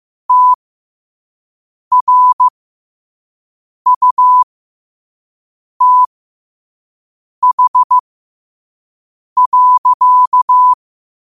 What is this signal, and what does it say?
Morse code, heard as a keyed tone, about 15 wpm, spaced as for 5 wpm, 1000 Hz. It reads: TRUTH.
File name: qotdfiles / 5wpm